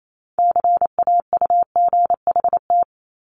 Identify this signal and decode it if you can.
Morse code, heard as a keyed tone, about 28 wpm, 700 Hz. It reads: CAUGHT
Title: Multi-letter Phonemes 28wpm